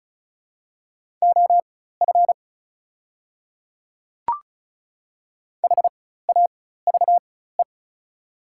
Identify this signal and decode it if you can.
Morse code, heard as a keyed tone, about 36 wpm, spaced as for 16 wpm, 690 Hz. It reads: OF HAVE
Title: 4x Character Spacing - Top 100 Words - 35wpm